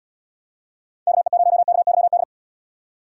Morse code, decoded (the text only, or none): KE0KYD